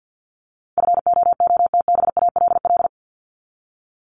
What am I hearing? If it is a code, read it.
Morse code, heard as a keyed tone, about 48 wpm, 700 Hz. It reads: FOOTBALL
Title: Top 1501-1750 Words 50wpm